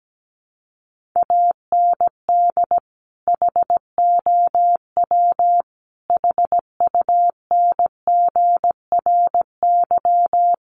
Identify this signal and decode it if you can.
Morse code, heard as a keyed tone, about 17 wpm, 700 Hz. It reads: AND HOW HUNGRY
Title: Sentences from Top 1400 Words 17wpm